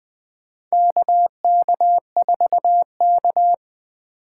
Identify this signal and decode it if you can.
Morse code, heard as a keyed tone, about 20 wpm, 700 Hz. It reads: KK4K
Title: USA Call Signs - 4 Characters - 3x Character Spacing - 20wpm